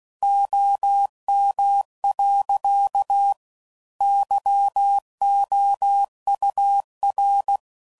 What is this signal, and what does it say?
Morse code, heard as a keyed tone, about 16 wpm, 800 Hz. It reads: OM. YOUR
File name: CW Code Groups @15WPM / QSOs / kc7sac